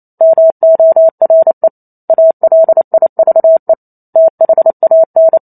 Morse code, heard as a keyed tone, about 28 wpm, 650 Hz. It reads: MORE ALIVE THAN
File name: Chapter 2 - 28 WPM